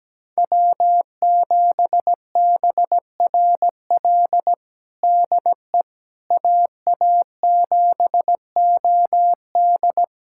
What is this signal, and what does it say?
Morse code, heard as a keyed tone, about 17 wpm, 700 Hz. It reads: W7BRL DE AA7OD
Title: USA Callsign DE Callsign - 5 Characters - 17wpm